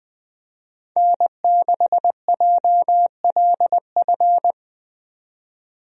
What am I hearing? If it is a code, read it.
Morse code, heard as a keyed tone, about 20 wpm, 700 Hz. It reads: N6JLF
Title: USA Call Signs - 5 Characters - 20wpm